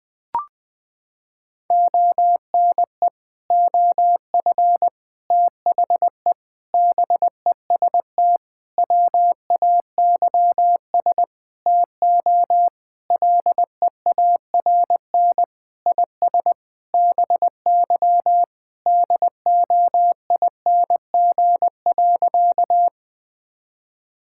Morse code, heard as a keyed tone, about 21 wpm, 700 Hz. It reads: ONE OF THE BEST WAYS TO LEARN IS BY DOING.
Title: Sentences with 6-Word Phrases - 20wpm